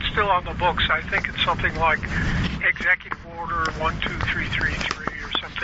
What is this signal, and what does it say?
Emergency Alert System (SAME) tones, heard as harsh buzzing bursts, with speech. It is heard from 0.9 to 4.7 s.